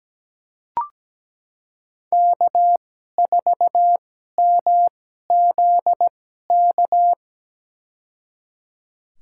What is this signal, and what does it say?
Morse code, heard as a keyed tone, about 17 wpm, spaced as for 12 wpm, 700 Hz. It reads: K4MZK
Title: USA Call Signs - 5 Characters - 2x Character Spacing - 17wpm